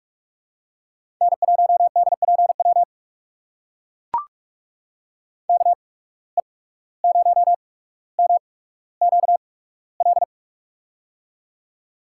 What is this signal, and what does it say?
Morse code, heard as a keyed tone, about 44 wpm, spaced as for 18 wpm, 700 Hz. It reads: N1DPW KE0MQL
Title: USA Call Signs - 8x Character Spacing - 45wpm